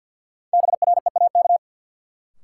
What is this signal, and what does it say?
Morse code, heard as a keyed tone, about 53 wpm, 700 Hz. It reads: BREAK
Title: Sentences from Top 900 Words - New Word Drill - 50wpm